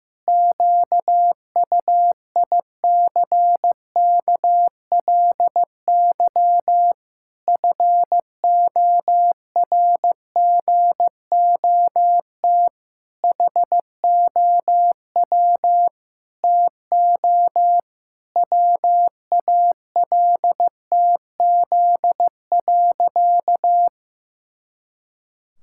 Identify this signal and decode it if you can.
Morse code, heard as a keyed tone, about 15 wpm, 700 Hz. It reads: QUICKLY FORGOT HOW TO WALTZ.
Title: Pangram Challenge v2 - 15wpm